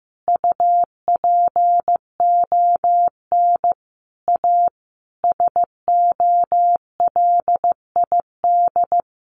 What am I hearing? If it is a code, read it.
Morse code, heard as a keyed tone, about 14 wpm, 700 Hz. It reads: UPON A SOLID
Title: Sentences from Top 1500 Words 15wpm